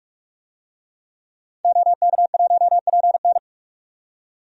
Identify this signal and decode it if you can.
Morse code, heard as a keyed tone, about 44 wpm, 700 Hz. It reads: OK1PN
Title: DXCC-100 Call Signs - 5 Characters - 45wpm